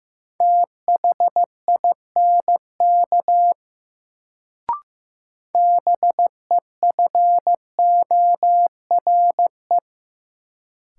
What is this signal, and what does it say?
Morse code, heard as a keyed tone, about 15 wpm, 700 Hz. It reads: THINK BEFORE
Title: Sentences from Top 200 Words - New Word Drill - 15wpm